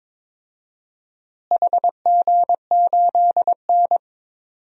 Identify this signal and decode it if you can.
Morse code, heard as a keyed tone, about 22 wpm, 700 Hz. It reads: HG8N
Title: DXCC-100 Call Signs - 4 Characters - Repeated 2x - 22wpm